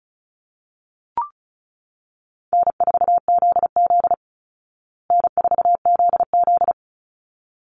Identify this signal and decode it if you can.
Morse code, heard as a keyed tone, about 34 wpm, 700 Hz. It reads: N4ZZ N4ZZ
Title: Easy Call Signs Repeated 2x - 35wpm